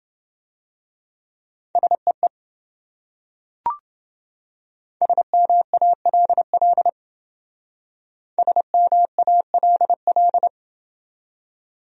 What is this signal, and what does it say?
Morse code, heard as a keyed tone, about 28 wpm, 700 Hz. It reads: SEE SMALL SMALL